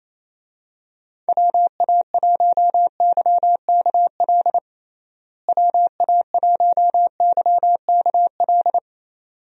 Morse code, heard as a keyed tone, about 28 wpm, 700 Hz. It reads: WA1YKL WA1YKL
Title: USA Call Signs - 6 Characters - Repeated 2x - 28wpm